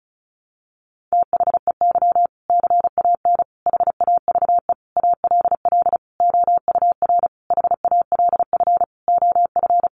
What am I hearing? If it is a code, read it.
Morse code, heard as a keyed tone, about 34 wpm, 700 Hz. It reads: THEY CAN HAVE ALL OUR HALF OF
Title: Sentences from Top 400 Words 35wpm